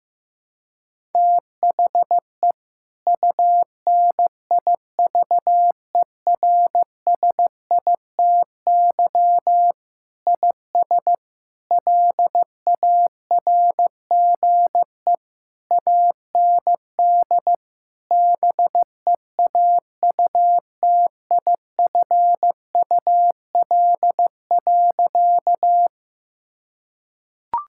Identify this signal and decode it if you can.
Morse code, heard as a keyed tone, about 15 wpm, 700 Hz. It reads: THE UNIVERSITY IS LARGE AND BEAUTIFUL.
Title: Sentences from Top 1000 Words - Review Part 5 - 15wpm